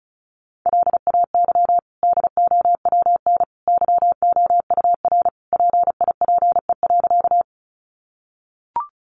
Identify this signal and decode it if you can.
Morse code, heard as a keyed tone, about 34 wpm, 700 Hz. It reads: LAY DOWN YOUR PIPE.
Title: Sentences from Top 1500 Words 35wpm